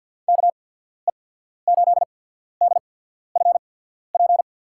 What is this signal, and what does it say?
Morse code, heard as a keyed tone, about 53 wpm, spaced as for 14 wpm, 700 Hz. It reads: KE8DFP
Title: USA Call Signs - 6 Characters - 8x Character Spacing - 50wpm